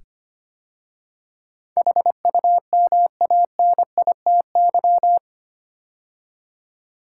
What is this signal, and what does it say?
Morse code, heard as a keyed tone, about 25 wpm, 700 Hz. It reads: HUMANITY
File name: 8 Letter Words - First 3 Letters Given - 25wpm